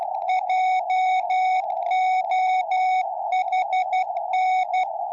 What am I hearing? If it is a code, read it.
Morse code, heard as a keyed tone, about 12 wpm, 740 Hz, with radio noise. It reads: JOHN
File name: tropo / gm4wja